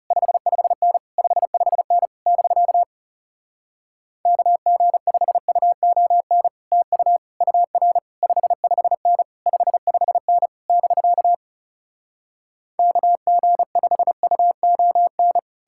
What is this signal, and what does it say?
Morse code, heard as a keyed tone, about 40 wpm, 700 Hz. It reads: 55N 55N <BK> KG5UON TU UR 55N 55N <BK> KG5UON